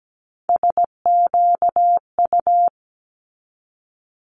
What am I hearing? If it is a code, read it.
Morse code, heard as a keyed tone, about 17 wpm, 700 Hz. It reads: SQU
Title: Multi-letter Phonemes 17wpm